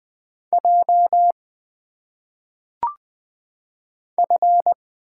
Morse code, heard as a keyed tone, about 20 wpm, 700 Hz. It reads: J F
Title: Single Letters - Phonetic Pronunciation 20wpm